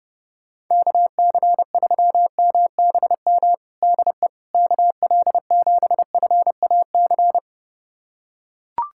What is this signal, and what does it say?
Morse code, heard as a keyed tone, about 30 wpm, 700 Hz. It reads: KC3MBM DE KL7FAC
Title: USA Callsign DE Callsign 30wpm